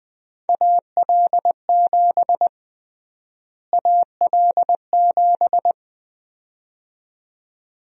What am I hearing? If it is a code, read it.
Morse code, heard as a keyed tone, about 20 wpm, 700 Hz. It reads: AL7 AL7
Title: USA Call Sign Prefixes - Repeated 2x - 20wpm